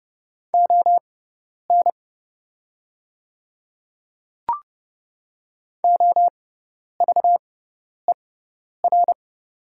Morse code, heard as a keyed tone, about 30 wpm, spaced as for 10 wpm, 700 Hz. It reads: ON OVER
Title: 6x Character Spacing - Top 100 Words - 30wpm